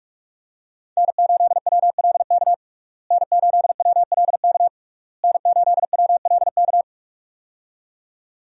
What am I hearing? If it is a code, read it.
Morse code, heard as a keyed tone, about 46 wpm, 700 Hz. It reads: N8WLK N8WLK N8WLK
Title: USA Call Signs - 5 Characters - Repeated 3x - 45wpm